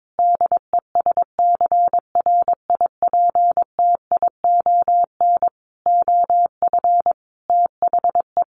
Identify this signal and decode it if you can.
Morse code, heard as a keyed tone, about 22 wpm, 700 Hz. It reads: DESCRIPTION OF THE
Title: Sentences from Top 1500 Words 22wpm